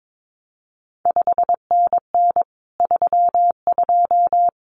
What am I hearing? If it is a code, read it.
Morse code, heard as a keyed tone, about 22 wpm, 700 Hz. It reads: 5NN 32
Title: CQ WW DX Contest Exchange - 22wpm